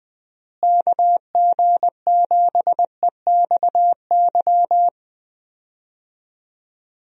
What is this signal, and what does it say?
Morse code, heard as a keyed tone, about 20 wpm, 700 Hz. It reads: KG7EXY
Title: USA Call Signs - 6 Characters - Repeated 2x - 20wpm